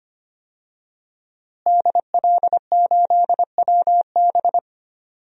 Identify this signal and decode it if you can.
Morse code, heard as a keyed tone, about 25 wpm, 700 Hz. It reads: DL8WB
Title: DXCC-100 Call Signs - 5 Characters - Repeated 2x - 25wpm